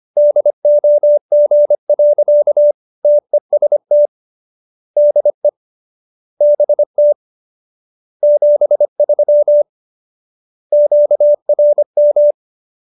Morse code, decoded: DOG. TEST DE BT 73 QRM

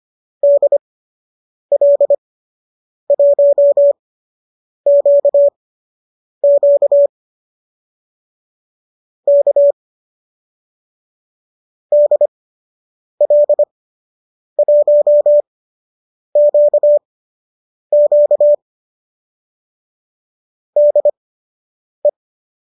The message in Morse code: DL1QQ K DL1QQ DE

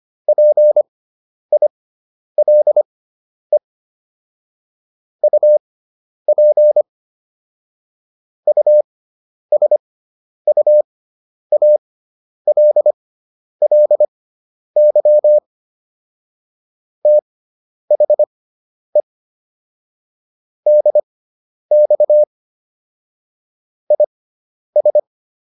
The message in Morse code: PILE UP USUALLY THE DX IS

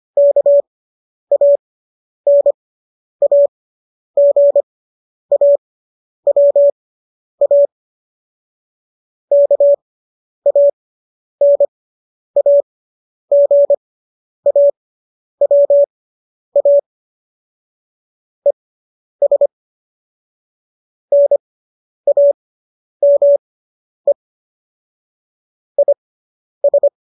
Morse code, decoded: KANAGAWA KANAGAWA ES NAME IS

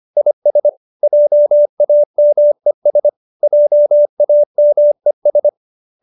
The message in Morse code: IS JAMES JAMES

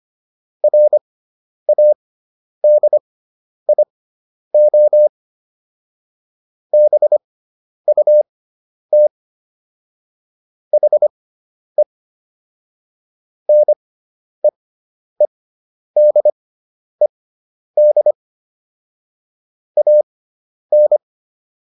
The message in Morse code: RADIO BUT HE NEEDED AN

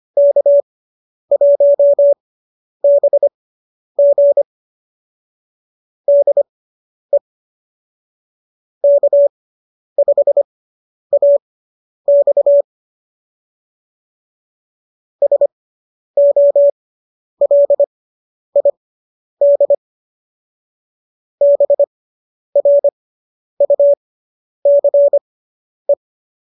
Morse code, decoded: K1BG DE K5AX SOLID BRUCE

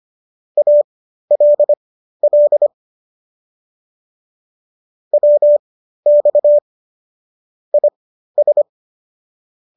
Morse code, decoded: ALL WX IS

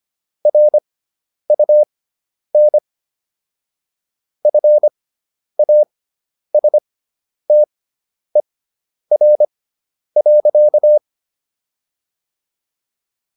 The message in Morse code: RUN FASTER.

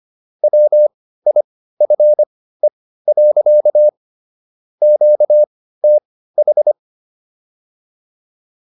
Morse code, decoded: WIFE. QTH

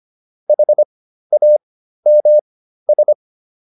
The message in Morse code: HAMS

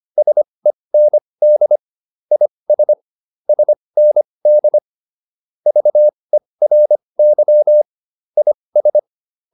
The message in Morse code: SEND IS SND VERY IS

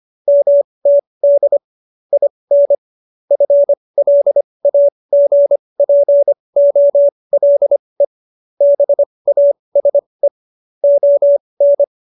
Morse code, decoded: MTD IN FLAGPOLE BASE ON

